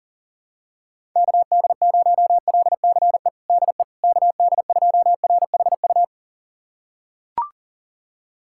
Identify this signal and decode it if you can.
Morse code, heard as a keyed tone, about 37 wpm, 700 Hz. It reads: KD0LCE DE KD2RHU
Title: USA Callsign DE Callsign - 6 Characters - 40wpm